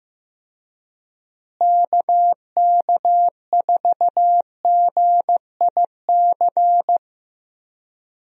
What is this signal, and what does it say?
Morse code, heard as a keyed tone, about 16 wpm, 700 Hz. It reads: KK4GIC
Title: USA Call Signs - 6 Characters - Repeated 2x - 15wpm